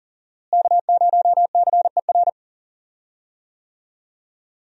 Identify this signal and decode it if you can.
Morse code, heard as a keyed tone, about 37 wpm, 700 Hz. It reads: K0CER
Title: USA Call Signs - 5 Characters - Repeated 2x - 40wpm